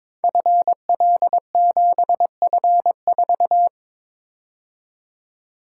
Morse code, heard as a keyed tone, about 22 wpm, 700 Hz. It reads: FL7F4
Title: Code Groups - 5 Letter:Numbers - 22wpm